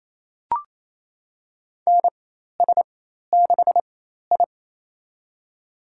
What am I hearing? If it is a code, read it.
Morse code, heard as a keyed tone, about 28 wpm, spaced as for 13 wpm, 700 Hz. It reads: NS6I
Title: USA Call Signs - 4 Characters - 4x Character Spacing - 28wpm